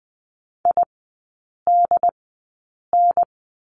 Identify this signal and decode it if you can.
Morse code, heard as a keyed tone, about 20 wpm, spaced as for 8 wpm, 700 Hz. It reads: IDN
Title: 121 Lesson 20e - Number 3 - Three Characters 20wpm